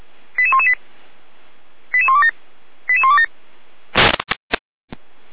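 Five-tone selective call (ZVEI), heard as quick runs of five tones, tones of about 80 ms. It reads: 88118 88127 88127
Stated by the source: ZVEI five-tone call 88191 received on SDR.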